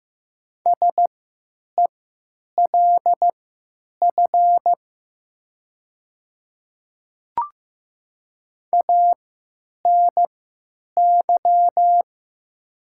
Morse code, heard as a keyed tone, about 15 wpm, spaced as for 9 wpm, 700 Hz. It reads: SELF ANY
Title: 3x Character Spacing - Top 200 Words - 15wpm